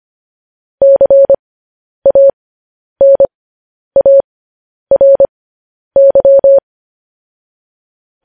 Morse code, heard as a keyed tone, about 25 wpm, spaced as for 10 wpm, 550 Hz. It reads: CANARY